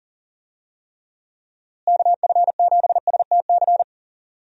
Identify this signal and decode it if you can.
Morse code, heard as a keyed tone, about 40 wpm, 700 Hz. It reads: KF7STC
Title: USA Call Signs - 6 Characters - Repeated 3x - 40wpm